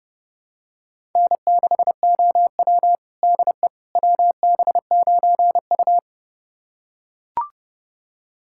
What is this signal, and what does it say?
Morse code, heard as a keyed tone, about 31 wpm, 700 Hz. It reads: N6OW DE WB9U
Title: USA Callsign DE Callsign - 4 Characters - 30wpm